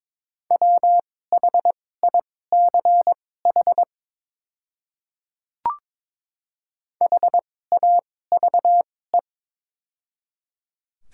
Morse code, heard as a keyed tone, about 22 wpm, spaced as for 16 wpm, 700 Hz. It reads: WHICH HAVE
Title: 2x Character Spacing - Top 100 Words - 22wpm